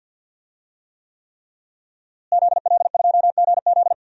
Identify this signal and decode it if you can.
Morse code, heard as a keyed tone, about 52 wpm, 700 Hz. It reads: ZL2GZ
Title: DXCC-100 Call Signs - 5 Characters - Repeated 3x - 50wpm